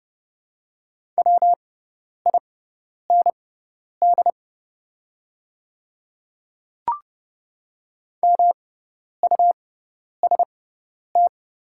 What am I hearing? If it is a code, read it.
Morse code, heard as a keyed tone, about 30 wpm, spaced as for 10 wpm, 700 Hz. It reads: WIND MUST
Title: 6x Character Spacing - Top 500 Words - 30wpm